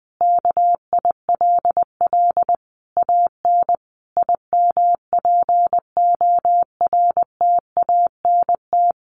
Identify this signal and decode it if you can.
Morse code, heard as a keyed tone, about 20 wpm, 700 Hz. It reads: KILL AN IMPORTANT
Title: A Sound Of Thunder 20wpm